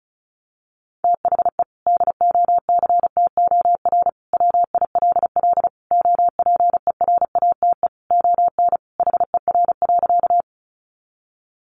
Morse code, heard as a keyed tone, about 34 wpm, 700 Hz. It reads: THE DOCTOR WILL OPERATE ON HER.
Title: Sentences from Top 1500 Words 35wpm